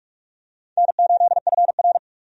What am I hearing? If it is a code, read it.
Morse code, heard as a keyed tone, about 46 wpm, 700 Hz. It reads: N8FR
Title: USA Call Signs - 4 Characters - 45wpm